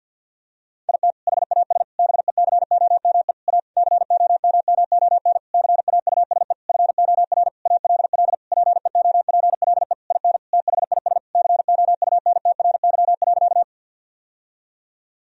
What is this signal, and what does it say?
Morse code, heard as a keyed tone, about 50 wpm, 700 Hz. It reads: IT HAS BECOME A COMMON CAUSE FOR ALL PEOPLE IN THIS COUNTRY.
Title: Sentences from Top 1000 Words - Review Part 3 - 50wpm